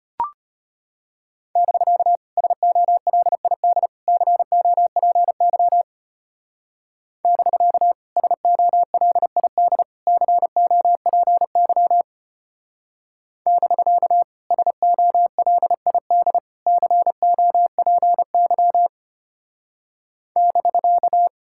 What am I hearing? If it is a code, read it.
Morse code, heard as a keyed tone, about 32 wpm, 700 Hz. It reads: <BK> SOLID COPY <BK> SOLID COPY <BK> SOLID COPY <BK>